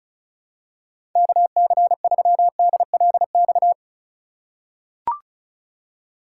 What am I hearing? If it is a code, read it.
Morse code, heard as a keyed tone, about 36 wpm, 700 Hz. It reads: KC3DLX
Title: USA Call Signs - 6 Characters - 3x Character Spacing - 35wpm